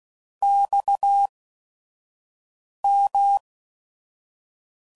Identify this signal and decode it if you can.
Morse code, heard as a keyed tone, about 16 wpm, spaced as for 5 wpm, 800 Hz. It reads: XM
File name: CW Code Groups @5WPM / QSOs / QSO WA7QOX 1